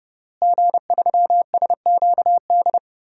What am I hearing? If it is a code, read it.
Morse code, heard as a keyed tone, about 30 wpm, 700 Hz. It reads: G3SQD
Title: DXCC-100 Call Signs - 5 Characters - Repeated 3x - 30wpm